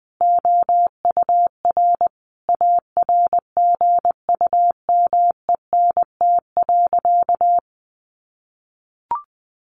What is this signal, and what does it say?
Morse code, heard as a keyed tone, about 20 wpm, 700 Hz. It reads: OUR ARGUMENT.
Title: Sentences from Top 1400 Words 20wpm